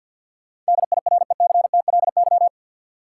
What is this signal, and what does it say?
Morse code, heard as a keyed tone, about 50 wpm, 700 Hz. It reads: DIRECTLY
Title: Sentences from Top 800 Words - New Word Drill - 50wpm